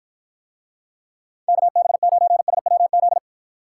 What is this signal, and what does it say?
Morse code, heard as a keyed tone, about 53 wpm, 700 Hz. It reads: KB9SWZ